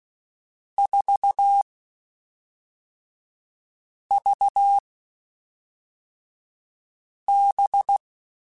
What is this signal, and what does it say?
Morse code, heard as a keyed tone, about 16 wpm, 800 Hz. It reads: 4 V B
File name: Groups@5 / Groups@5 mp3s / VB46@5